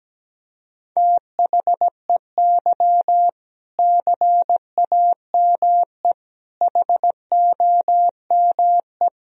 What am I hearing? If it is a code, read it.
Morse code, heard as a keyed tone, about 17 wpm, 700 Hz. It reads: THEY CAME HOME
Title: Sentences from Top 1000 Words - Review Part 3 - 17wpm